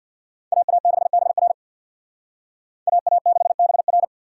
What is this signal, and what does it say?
Morse code, heard as a keyed tone, about 64 wpm, 700 Hz. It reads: AA6BR AA6BR